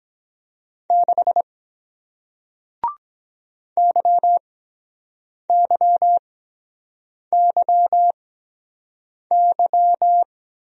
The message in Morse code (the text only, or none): 6 YYYY